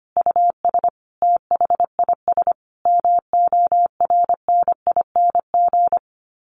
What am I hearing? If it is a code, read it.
Morse code, heard as a keyed tone, about 25 wpm, 700 Hz. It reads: US THIS MORNING
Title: Sets of 5 Words Encore 25wpm